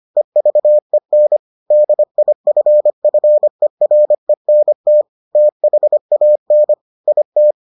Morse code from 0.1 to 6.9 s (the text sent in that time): EVEN DIFFERENT THAN